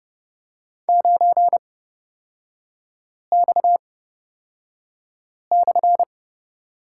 Morse code, heard as a keyed tone, about 30 wpm, spaced as for 5 wpm, 690 Hz. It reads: 9X/
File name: Lesson 37e - Letter x - Three Characters 30wpm